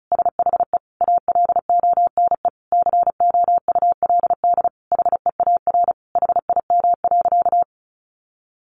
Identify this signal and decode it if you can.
Morse code, heard as a keyed tone, about 34 wpm, 700 Hz. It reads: SHE ALONE COULD HEAR HIM.
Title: Sentences from Top 1000 Words 35wpm